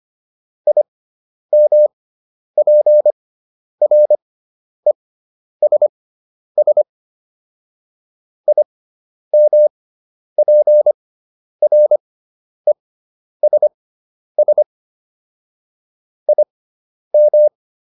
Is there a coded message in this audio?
Yes: Morse code from 0.7 to 17.5 s, heard as a keyed tone, about 25 wpm, spaced as for 10 wpm, 600 Hz. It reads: IMPRESS IMPRESS IM